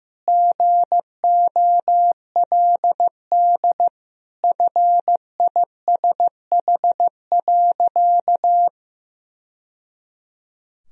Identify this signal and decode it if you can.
Morse code, heard as a keyed tone, about 15 wpm, 700 Hz. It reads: GOLD FISH.
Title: Sentences from Top 1000 Words - Review Part 3 - 15wpm